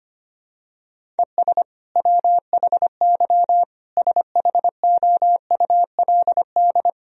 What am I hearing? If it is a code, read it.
Morse code, heard as a keyed tone, about 25 wpm, 700 Hz. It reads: ES WHY SHOULD
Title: Sentences from Top 300 Words and Abbreviations 25wpm